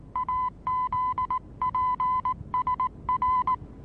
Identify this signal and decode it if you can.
Morse code, heard as a keyed tone, about 19 wpm, 1010 Hz. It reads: AZPSR
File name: repos / Morsekurs / DJ4UF Morsen26